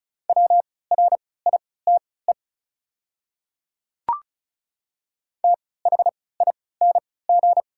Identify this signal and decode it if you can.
Morse code, heard as a keyed tone, about 36 wpm, spaced as for 20 wpm, 690 Hz. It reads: WRITE THING
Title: 3x Character Spacing - Top 100 Words - 35wpm